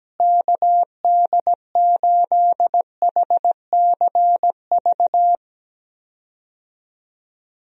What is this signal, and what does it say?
Morse code, heard as a keyed tone, about 17 wpm, 700 Hz. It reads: KD8HCV
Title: USA Call Signs - 6 Characters - Repeated 2x - 17wpm